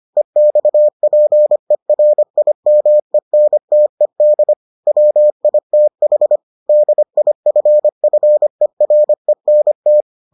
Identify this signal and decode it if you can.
Morse code, heard as a keyed tone, about 25 wpm, 600 Hz. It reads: EXPERIMENTED WITH DIFFERENT